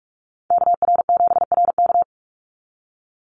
Morse code, heard as a keyed tone, about 44 wpm, 700 Hz. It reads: KR7RK
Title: NA SOTA Activator Callsigns 45wpm